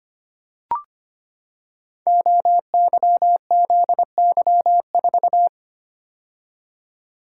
Morse code, heard as a keyed tone, about 25 wpm, 700 Hz. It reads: OYZY4
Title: Code Groups - 5 Letter:Numbers - 25wpm